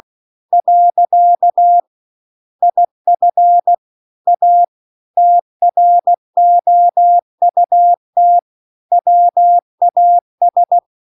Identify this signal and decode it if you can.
Morse code, heard as a keyed tone, about 16 wpm, 700 Hz. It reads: . IF A TROUT WAS
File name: CW-The-Big-Twoheaded-River-2-16WPM